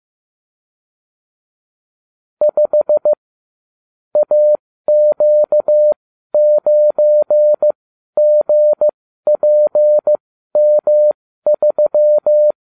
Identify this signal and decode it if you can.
Morse code, heard as a keyed tone, about 15 wpm, spaced as for 12 wpm, 600 Hz. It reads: 5 AQ9GPM3